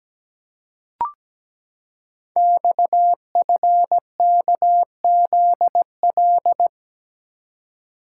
Morse code, heard as a keyed tone, about 17 wpm, 700 Hz. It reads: XFKZL
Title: Code Groups - 5 Letters - 17wpm